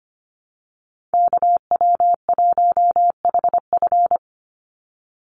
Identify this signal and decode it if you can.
Morse code, heard as a keyed tone, about 25 wpm, 700 Hz. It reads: KW1HF
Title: USA Call Signs 25wpm